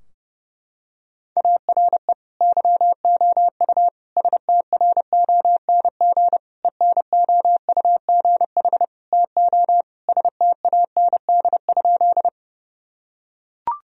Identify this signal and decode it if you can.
Morse code, heard as a keyed tone, about 30 wpm, 700 Hz. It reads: ARE YOU STRONG ENOUGH TO STAND?
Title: Sentences from Top 1000 Words - Review Part 3 - 30wpm